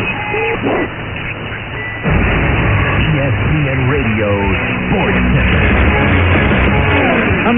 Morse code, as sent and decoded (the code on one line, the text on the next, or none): none